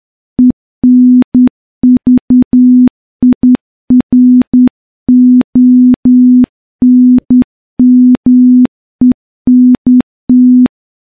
Morse code, heard as a keyed tone, about 10 wpm, 250 Hz. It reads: ENVIRONMENT
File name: morse-training-10WPM